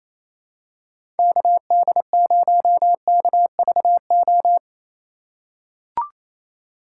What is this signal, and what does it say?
Morse code, heard as a keyed tone, about 28 wpm, 700 Hz. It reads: KD0KVO
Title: USA Call Signs - 6 Characters - 4x Character Spacing - 28wpm